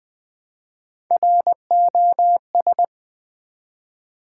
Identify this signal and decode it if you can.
Morse code, heard as a keyed tone, about 20 wpm, 700 Hz. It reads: ROS
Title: Build A Word - 1 Letter At A Time - 4-Letter Words - 20wpm